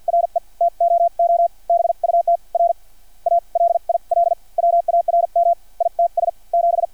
Morse code, heard as a keyed tone, about 50 wpm, spaced as for 42 wpm, 680 Hz. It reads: RE TOO BUT R A LIL WARM ITS 7
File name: archived / CRSnet